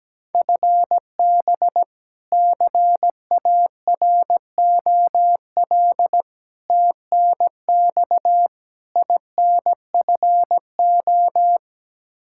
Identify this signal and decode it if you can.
Morse code, heard as a keyed tone, about 17 wpm, 700 Hz. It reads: FB CAROL TNX INFO
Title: K1USN SST QSO Practice 17wpm